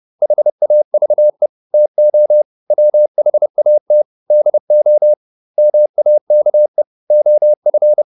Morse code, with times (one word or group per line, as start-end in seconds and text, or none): 0.2-1.5 HAVE
1.7-2.4 TO
2.7-4.0 WHAT
4.3-5.1 DO
5.6-6.8 MAKE
7.1-8.0 OF